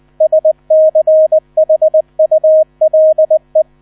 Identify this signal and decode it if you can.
Morse code, heard as a keyed tone, about 20 wpm, 630 Hz, with radio noise. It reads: SCHULE
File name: 000 Vorwort 20wpm